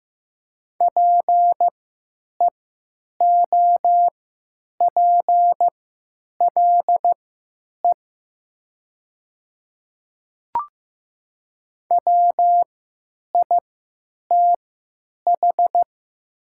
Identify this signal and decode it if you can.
Morse code, heard as a keyed tone, about 15 wpm, spaced as for 9 wpm, 700 Hz. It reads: PEOPLE WITH
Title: 3x Character Spacing - Top 100 Words - 15wpm